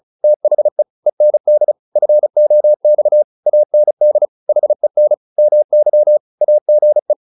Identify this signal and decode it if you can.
Morse code, heard as a keyed tone, about 34 wpm, 600 Hz. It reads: THE END FOX AND HEN MY AGE